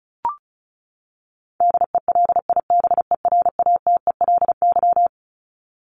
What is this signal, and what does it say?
Morse code, heard as a keyed tone, about 34 wpm, 700 Hz. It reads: DELIBERATELY
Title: Top 1751-2000 Words 35wpm